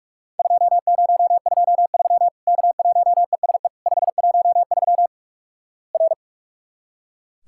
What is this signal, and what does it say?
Morse code, heard as a keyed tone, about 41 wpm, 700 Hz. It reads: 2023 K1ESE 513 R
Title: Worked All Europe DX Contest - QTC Exchange - 45wpm